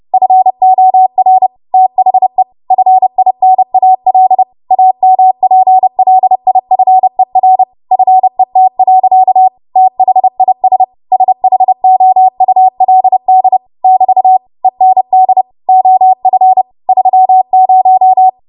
Morse code, heard as a keyed tone, about 30 wpm, 750 Hz. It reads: FOR THE FINAL AMPLIFER FET. THIS SHOULD <BT> END OF 30
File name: Archive / 30 WPM / 30WPM